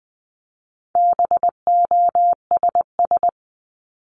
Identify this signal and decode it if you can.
Morse code, heard as a keyed tone, about 20 wpm, 700 Hz. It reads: BOSS
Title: Top 501-750 Words 20wpm